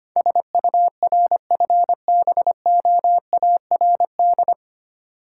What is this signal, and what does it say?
Morse code, heard as a keyed tone, about 25 wpm, 700 Hz. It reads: SURFBOARD
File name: practice / 25wpm / Difficult Words with Hints - 25wpm